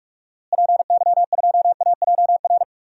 Morse code, heard as a keyed tone, about 41 wpm, 700 Hz. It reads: PY2AJR
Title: DXCC-100 Call Signs - 6 Characters - Repeated 3x - 45wpm